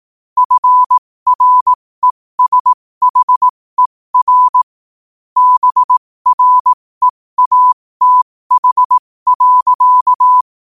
Morse code, heard as a keyed tone, about 18 wpm, spaced as for 15 wpm, 1000 Hz. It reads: FRESHER BREATH.
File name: qotdfiles / 15wpm